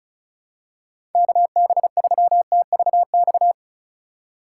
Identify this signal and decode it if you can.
Morse code, heard as a keyed tone, about 36 wpm, 700 Hz. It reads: KB3TVX
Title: USA Call Signs - 3x Character Spacing - 35wpm